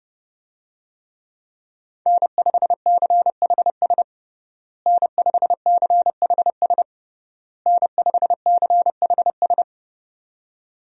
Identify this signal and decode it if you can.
Morse code, heard as a keyed tone, about 30 wpm, 700 Hz. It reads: N5CHS N5CHS N5CHS
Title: USA Call Signs - 5 Characters - Repeated 3x - 30wpm